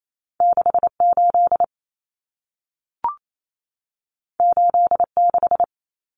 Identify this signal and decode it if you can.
Morse code, heard as a keyed tone, about 28 wpm, 700 Hz. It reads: 68 86
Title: Common Cut Numbers - 2-Digit 28wpm